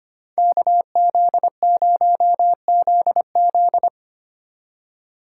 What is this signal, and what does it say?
Morse code, heard as a keyed tone, about 32 wpm, 700 Hz. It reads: KZ0ZZ